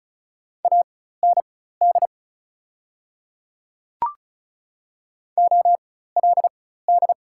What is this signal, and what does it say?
Morse code, heard as a keyed tone, about 34 wpm, spaced as for 16 wpm, 700 Hz. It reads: AND OLD
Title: 4x Character Spacing - Top 200 Words - 35wpm